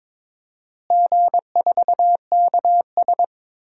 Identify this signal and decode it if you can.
Morse code, heard as a keyed tone, about 21 wpm, 700 Hz. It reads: G4KS
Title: DXCC-100 Call Signs - 4 Characters - Repeated 2x - 22wpm